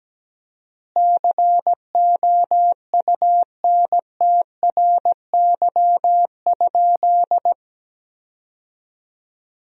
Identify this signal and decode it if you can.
Morse code, heard as a keyed tone, about 17 wpm, 700 Hz. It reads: COUNTRY?
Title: 4x Word-Spacing - Sentences from Top 300 Words - 17wpm